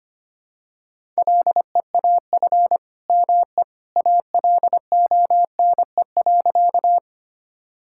Morse code, heard as a keyed tone, about 25 wpm, 700 Hz. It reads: LEAF ME ALONE.
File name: practice / 25wpm / All-Age Appropriate Jokes - 25wpm